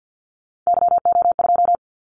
Morse code, heard as a keyed tone, about 50 wpm, 700 Hz. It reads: YO2
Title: Callsign Prefixes for Top 25 DXCC - 50wpm